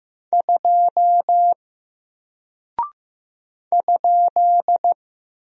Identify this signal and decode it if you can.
Morse code, heard as a keyed tone, about 15 wpm, 700 Hz. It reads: 2 ?
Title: Single Character (Letter:Number:Punctuation:Prosign) 15wpm